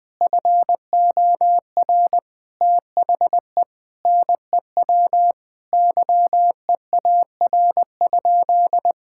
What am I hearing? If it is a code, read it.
Morse code, heard as a keyed tone, about 20 wpm, 700 Hz. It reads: FOR THE NEW YEAR?
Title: Sentences from Top 1000 Words - Review Part 5 - 20wpm